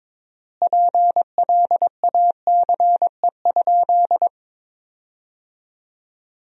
Morse code, heard as a keyed tone, about 22 wpm, 700 Hz. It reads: PLACE?
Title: 4x Word-Spacing - Sentences from Top 200 Words - 22wpm